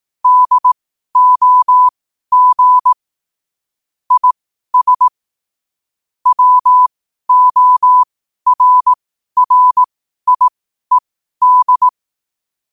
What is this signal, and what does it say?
Morse code, heard as a keyed tone, about 18 wpm, spaced as for 12 wpm, 1000 Hz. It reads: DOG IS WORRIED
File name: qotdfiles / 12wpm